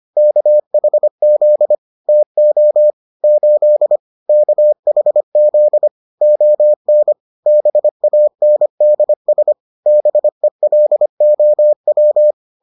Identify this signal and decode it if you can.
Morse code, heard as a keyed tone, about 25 wpm, 600 Hz. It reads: KHZ TO 8 KHZ ON BANDS BELOW